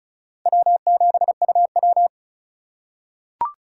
Morse code, heard as a keyed tone, about 36 wpm, 700 Hz. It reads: W7UW
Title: USA Call Signs - 4 Characters - 4x Character Spacing - 35wpm